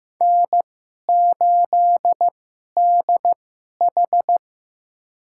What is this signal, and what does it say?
Morse code, heard as a keyed tone, about 15 wpm, spaced as for 11 wpm, 700 Hz. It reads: N8DH
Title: USA Call Signs - 4 Characters - 2x Character Spacing - 15wpm